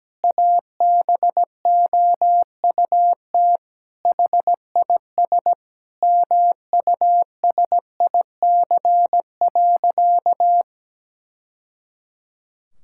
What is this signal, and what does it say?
Morse code, heard as a keyed tone, about 17 wpm, 700 Hz. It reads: ABOUT HIS MUSIC.
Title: Sentences from Top 1000 Words - Review Part 6 - 17wpm